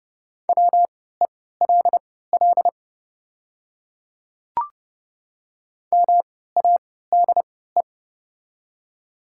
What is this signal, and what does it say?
Morse code, heard as a keyed tone, about 30 wpm, spaced as for 17 wpm, 700 Hz. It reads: WELL MADE
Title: 3x Character Spacing - Top 200 Words - 30wpm